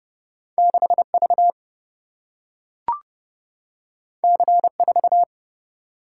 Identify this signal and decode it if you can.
Morse code, heard as a keyed tone, about 30 wpm, 700 Hz. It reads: 6 V C 4
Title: Single-Letter Single-Number 30wpm